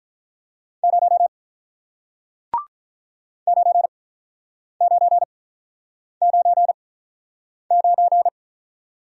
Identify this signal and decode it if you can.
Morse code, heard as a keyed tone, about 47 wpm, spaced as for 9 wpm, 700 Hz. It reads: 0 9999